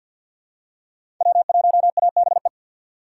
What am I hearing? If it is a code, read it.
Morse code, heard as a keyed tone, about 52 wpm, 700 Hz. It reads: W1ABE
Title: USA Call Signs - 5 Characters - 6x Character Spacing - 50wpm